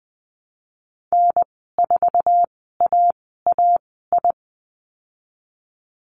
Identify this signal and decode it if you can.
Morse code, heard as a keyed tone, about 19 wpm, spaced as for 14 wpm, 700 Hz. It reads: N4AAI
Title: 048 Lesson 9g - Number 4 - Call Signs 20wpm